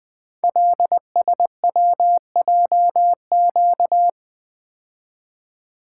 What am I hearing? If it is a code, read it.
Morse code, heard as a keyed tone, about 20 wpm, 700 Hz. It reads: LSWJQ
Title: Code Groups - 5 Letter:Numbers - 20wpm